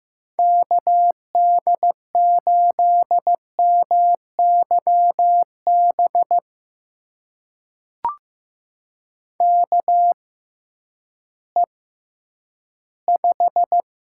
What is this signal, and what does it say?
Morse code, heard as a keyed tone, about 15 wpm, spaced as for 10 wpm, 700 Hz. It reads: KD8MYB KE5
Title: USA Call Signs - 6 Characters - 6x Character Spacing - 15wpm